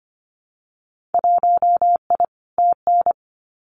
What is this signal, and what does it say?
Morse code, heard as a keyed tone, about 25 wpm, 700 Hz. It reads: 1I TN
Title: Winter Field Day Exchanges Encore 25wpm